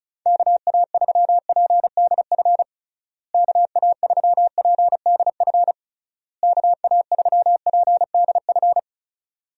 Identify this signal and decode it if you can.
Morse code, heard as a keyed tone, about 36 wpm, 700 Hz. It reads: KA3PDF KA3PDF KA3PDF
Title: USA Call Signs - 6 Characters - Repeated 3x - 35wpm